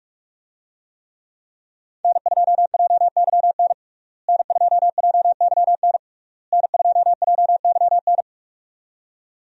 Morse code, heard as a keyed tone, about 48 wpm, 700 Hz. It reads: N2JYN N2JYN N2JYN
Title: USA Call Signs - 5 Characters - Repeated 3x - 45wpm